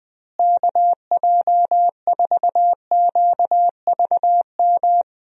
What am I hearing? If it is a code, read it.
Morse code, heard as a keyed tone, about 20 wpm, 700 Hz. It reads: KJ4QVM
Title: USA Call Signs - 6 Characters - Repeated 2x - 20wpm